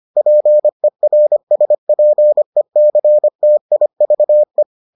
Morse code, heard as a keyed tone, about 25 wpm, 600 Hz. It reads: PERSPECTIVE